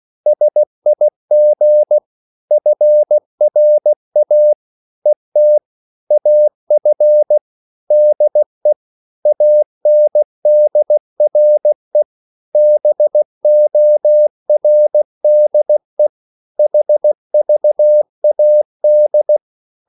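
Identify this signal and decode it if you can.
Morse code, heard as a keyed tone, about 16 wpm, 600 Hz. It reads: SIG FRA ET AF DE ANDRE BORDE HVAD
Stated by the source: Slow (QRS) Morse code at 16 wpm, from CW med Gnister.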